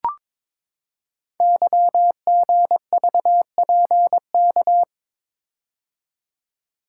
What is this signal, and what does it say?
Morse code, heard as a keyed tone, about 22 wpm, 700 Hz. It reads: YGVPK
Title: Code Groups - 5 Letters - 22wpm